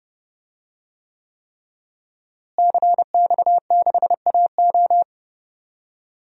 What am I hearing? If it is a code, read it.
Morse code, heard as a keyed tone, about 30 wpm, 700 Hz. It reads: CX6AO
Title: DXCC-100 Call Signs - 5 Characters - Repeated 3x - 30wpm